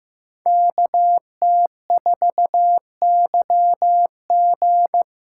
Morse code, heard as a keyed tone, about 18 wpm, 700 Hz. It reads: KT4YG
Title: Speed Racing - USA 2x2 Call Signs 15wpm